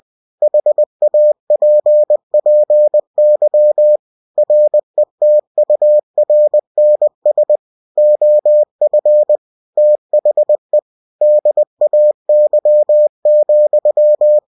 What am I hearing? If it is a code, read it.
Morse code, heard as a keyed tone, about 20 wpm, 600 Hz. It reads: HAPPY RETURNS OF THE DAY,
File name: cw-books / winnie-the-pooh / winnie-20wpm-page-0047